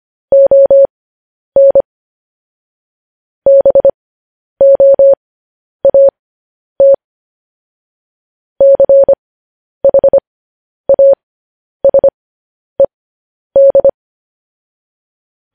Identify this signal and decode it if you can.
Morse code, heard as a keyed tone, about 25 wpm, spaced as for 10 wpm, 550 Hz. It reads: ON BOAT CHASED